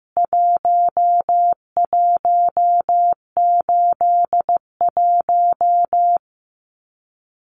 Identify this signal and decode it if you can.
Morse code, heard as a keyed tone, about 15 wpm, 700 Hz. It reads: 1181
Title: CWT Exchange 15wpm